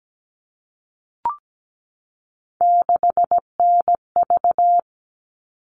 Morse code, heard as a keyed tone, about 17 wpm, 700 Hz. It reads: 6NV
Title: All Cut Numbers - 3-Digit 17wpm